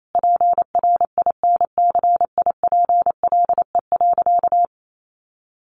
Morse code, heard as a keyed tone, about 28 wpm, 700 Hz. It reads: PRINCIPLE.
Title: Sentences from Top 1400 Words 28wpm